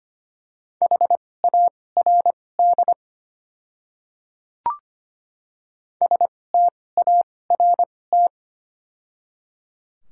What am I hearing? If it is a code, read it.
Morse code, heard as a keyed tone, about 25 wpm, spaced as for 18 wpm, 700 Hz. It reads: HARD START
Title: 2x Character Spacing - Top 500 Words - 25wpm